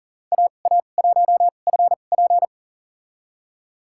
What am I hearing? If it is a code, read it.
Morse code, heard as a keyed tone, about 38 wpm, spaced as for 28 wpm, 700 Hz. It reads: AA1FP
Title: USA Call Signs - 5 Characters - 2x Character Spacing - 40wpm